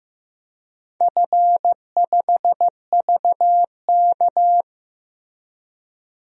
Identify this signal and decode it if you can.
Morse code, heard as a keyed tone, about 15 wpm, 700 Hz. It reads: F5VK
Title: DXCC-100 Call Signs - 4 Characters - Repeated 2x - 15wpm